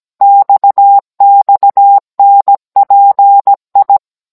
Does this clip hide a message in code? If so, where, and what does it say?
Yes: Morse code from 0.2 to 4.0 s, heard as a keyed tone, about 17 wpm, 800 Hz. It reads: XXNPI